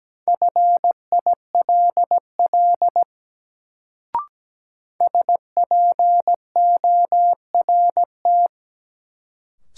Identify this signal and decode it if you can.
Morse code, heard as a keyed tone, about 17 wpm, 700 Hz. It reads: FILL SPORT
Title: Sentences from Top 1500 Words - New Word Drill Alt - 17wpm